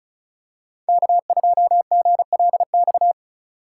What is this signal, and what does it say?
Morse code, heard as a keyed tone, about 34 wpm, 700 Hz. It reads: K2GLX
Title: USA Call Signs - 5 Characters - 2x Character Spacing - 35wpm